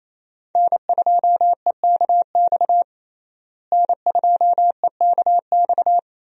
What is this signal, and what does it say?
Morse code, heard as a keyed tone, about 28 wpm, 700 Hz. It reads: N2EKX N2EKX
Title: USA Call Signs - 5 Characters - Repeated 2x - 28wpm